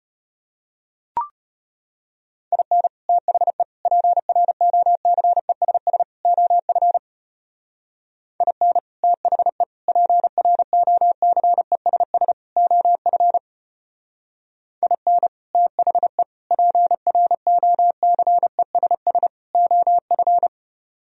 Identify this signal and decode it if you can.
Morse code, heard as a keyed tone, about 31 wpm, 700 Hz. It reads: IN THE PROCESS OF IN THE PROCESS OF IN THE PROCESS OF
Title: Speed-Racing - Sets of 4 Words 25wpm